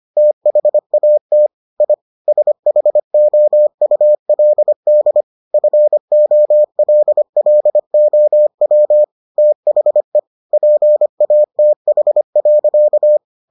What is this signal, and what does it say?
Morse code, heard as a keyed tone, about 25 wpm, 600 Hz. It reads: THAT I SHOULD FOLLOW THE PATH.